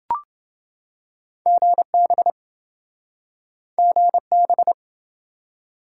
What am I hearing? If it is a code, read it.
Morse code, heard as a keyed tone, about 28 wpm, 700 Hz. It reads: GB GB